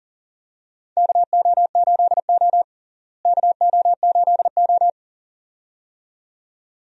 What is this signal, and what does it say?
Morse code, heard as a keyed tone, about 38 wpm, 700 Hz. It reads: KO8O KO8O
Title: USA Call Signs - 4 Characters - Repeated 2x - 40wpm